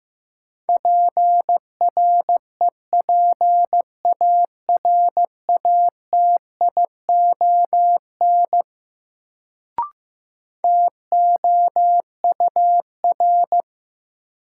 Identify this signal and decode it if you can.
Morse code, heard as a keyed tone, about 15 wpm, 700 Hz. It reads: PREPARATION TOUR
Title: Sentences from Top 1400 Words - New Word Drill - 15wpm